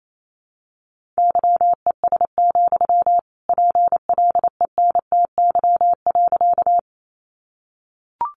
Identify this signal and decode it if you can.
Morse code, heard as a keyed tone, about 28 wpm, 700 Hz. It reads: YES, PLENTY.
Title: Sentences from Top 1300 Words 28wpm